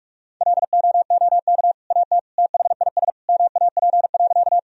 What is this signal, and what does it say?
Morse code, heard as a keyed tone, about 46 wpm, 700 Hz. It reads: LOOK AT THIS MAP.
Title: Sentences from Top 1000 Words - Review Part 2 - 45wpm